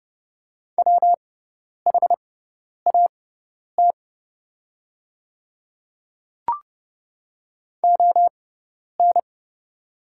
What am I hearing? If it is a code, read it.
Morse code, heard as a keyed tone, about 31 wpm, spaced as for 10 wpm, 690 Hz. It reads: WHAT ON
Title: 6x Character Spacing - Top 200 Words - 30wpm